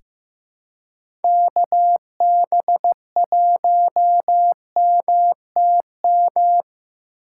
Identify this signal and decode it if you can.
Morse code, heard as a keyed tone, about 15 wpm, 700 Hz. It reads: KB1MTM
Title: USA Call Signs - 4x Character Spacing - 15wpm